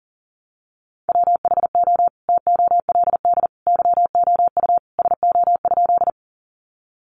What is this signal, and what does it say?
Morse code, heard as a keyed tone, about 40 wpm, 700 Hz. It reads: WHO TOLD YOU SO?
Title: Sentences from Top 400 Words 40wpm